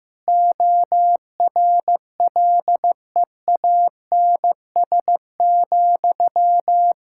Morse code, heard as a keyed tone, about 16 wpm, 700 Hz. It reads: ORLEANS,
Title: Practice QSO - Exchange 1 - Beginning - 15wpm